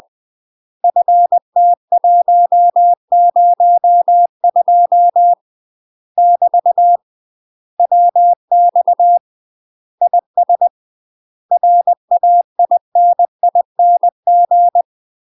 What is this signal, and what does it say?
Morse code, heard as a keyed tone, about 20 wpm, 700 Hz. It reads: FT102 <BT> WX IS RAINING
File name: UK MASTER QSOs TRAINING / 20WPM / UK MASTER QSOs 0068